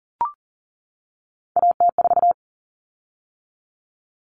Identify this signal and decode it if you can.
Morse code, heard as a keyed tone, about 40 wpm, 700 Hz. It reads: AT4
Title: Common Cut Numbers - 3-Digit 40wpm